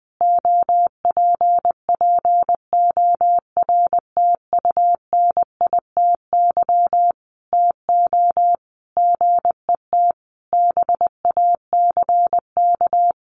Morse code, heard as a keyed tone, about 20 wpm, 700 Hz. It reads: OPPORTUNITY TO GET BACK
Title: Sentences from Top 1400 Words 20wpm